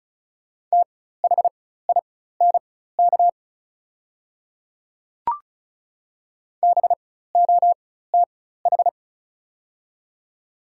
Morse code, heard as a keyed tone, about 34 wpm, spaced as for 16 wpm, 700 Hz. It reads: THINK BOTH
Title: 4x Character Spacing - Top 300 Words - 35wpm